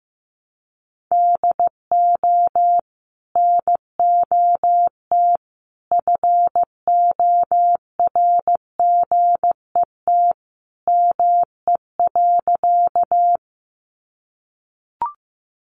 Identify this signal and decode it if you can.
Morse code, heard as a keyed tone, about 15 wpm, 700 Hz. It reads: DO NOT FORGET ME.
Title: Sentences from Top 1200 Words 15wpm